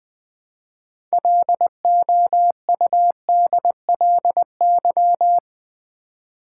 Morse code, heard as a keyed tone, about 20 wpm, 700 Hz. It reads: LOUDLY
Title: 6 Letter Words - First 3 Letters Given - 20wpm